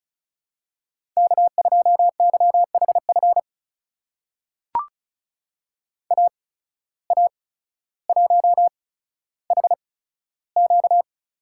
Morse code, heard as a keyed tone, about 36 wpm, spaced as for 15 wpm, 700 Hz. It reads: K2YHF AA1HQ
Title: USA Call Signs - 5 Characters - 8x Character Spacing - 35wpm